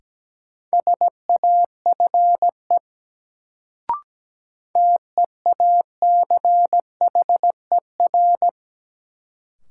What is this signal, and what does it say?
Morse code, heard as a keyed tone, about 17 wpm, 700 Hz. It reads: SAFE TEACHER
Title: Sentences from Top 900 Words - New Word Drill - 17wpm